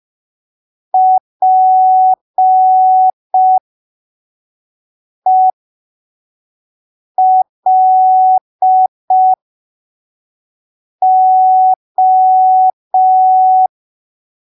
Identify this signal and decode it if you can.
Morse code, heard as a keyed tone, about 5 wpm, spaced as for 3 wpm, 750 Hz. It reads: PELO
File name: arquivos / texto / texto 21 05ppm